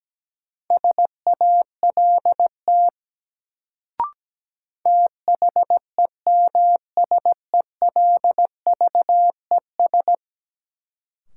Morse code, heard as a keyed tone, about 17 wpm, 700 Hz. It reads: SALT THEMSELVES
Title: Sentences from Top 800 Words - New Word Drill - 17wpm